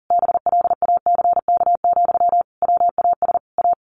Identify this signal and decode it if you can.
Morse code, heard as a keyed tone, about 40 wpm, 700 Hz. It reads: BLACK, WAS A
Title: A Sound Of Thunder 40wpm